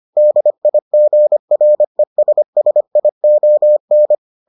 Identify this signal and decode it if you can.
Morse code, heard as a keyed tone, about 25 wpm, 600 Hz. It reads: DIGRESSION